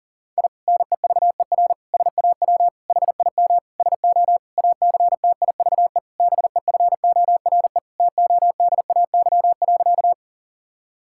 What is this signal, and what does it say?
Morse code, heard as a keyed tone, about 37 wpm, 700 Hz. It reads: I NEVER SAW HIM SO ACTIVE BEFORE TODAY.
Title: Sentences from Top 1000 Words - Review Part 6 - 40wpm